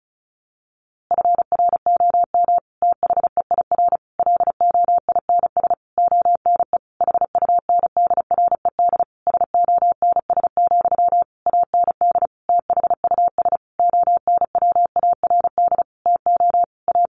Morse code, heard as a keyed tone, about 34 wpm, 700 Hz. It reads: FROM THEIR LOINS ONE HUNDRED SONS, AND THUS ONWARD TO A
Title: A Sound Of Thunder 35wpm